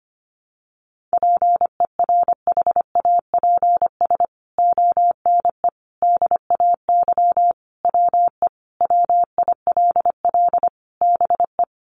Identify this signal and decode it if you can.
Morse code, heard as a keyed tone, about 25 wpm, 700 Hz. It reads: PERHAPS ONE DAY WE WILL BE
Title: Sentences from Top 600 Words 25wpm